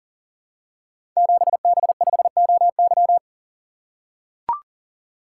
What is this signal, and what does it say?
Morse code, heard as a keyed tone, about 42 wpm, 700 Hz. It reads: 7B5OY
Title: Code Groups - 5 Letter:Numbers - 40wpm